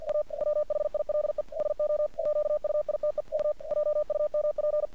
Morse code, heard as a keyed tone, about 46 wpm, 600 Hz. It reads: A WHILE SO QUITE U JUMP
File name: archived / CRSnet